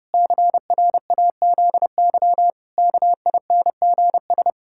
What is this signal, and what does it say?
Morse code, heard as a keyed tone, about 30 wpm, 700 Hz. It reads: CRAZY KINGS
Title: Pangram Challenge v2 - 30wpm